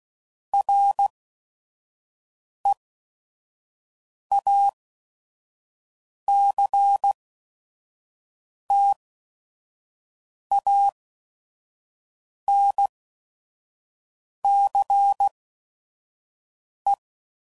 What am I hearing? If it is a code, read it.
Morse code, heard as a keyed tone, about 16 wpm, spaced as for 5 wpm, 800 Hz. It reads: REACTANCE
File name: CW Code Groups @5WPM / Technical Subjects / Reactance